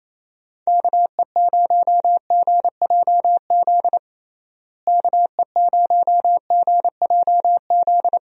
Morse code, heard as a keyed tone, about 28 wpm, 700 Hz. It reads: KE0GJZ KE0GJZ
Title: USA Call Signs - 6 Characters - Repeated 2x - 28wpm